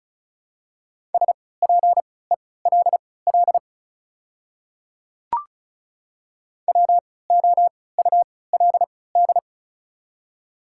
Morse code, heard as a keyed tone, about 36 wpm, spaced as for 20 wpm, 700 Hz. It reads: SPELL WOULD
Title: 3x Character Spacing - Top 200 Words - 35wpm